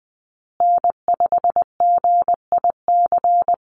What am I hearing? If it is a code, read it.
Morse code, heard as a keyed tone, about 20 wpm, 700 Hz. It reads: N5GIC
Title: USA Call Signs Repeated 2x 20wpm